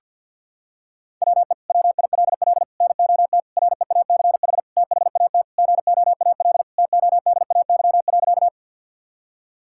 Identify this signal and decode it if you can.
Morse code, heard as a keyed tone, about 52 wpm, 700 Hz. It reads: WE WILL NOT REACH THAT GOAL TODAY.
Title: Sentences from Top 1000 Words - Review Part 5 - 50wpm